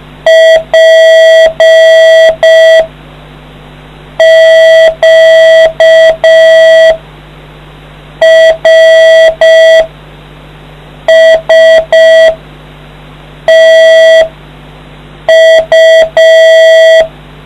Morse code, heard as a keyed tone, about 5 wpm, 650 Hz, with noise. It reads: P Q R S T U